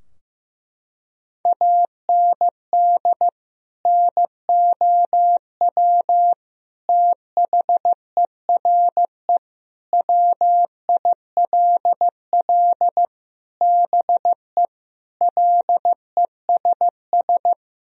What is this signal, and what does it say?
Morse code, heard as a keyed tone, about 15 wpm, 700 Hz. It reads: AND NOW THERE WILL BE LESS
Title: Sentences from Top 1000 Words - Review Part 2 - 15wpm